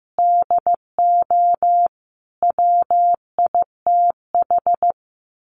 Morse code, heard as a keyed tone, about 15 wpm, 700 Hz. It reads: DO WITH
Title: Sets of 5 Words Encore 15wpm